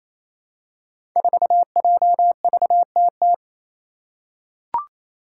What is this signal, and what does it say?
Morse code, heard as a keyed tone, about 27 wpm, 700 Hz. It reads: 4JVTT
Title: Five Challenging Letter-Number Combinations 28wpm